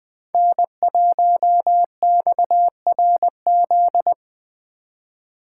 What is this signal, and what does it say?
Morse code, heard as a keyed tone, about 20 wpm, 700 Hz. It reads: N1XRZ
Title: Speed Racing - USA 1x3 Call Signs 17wpm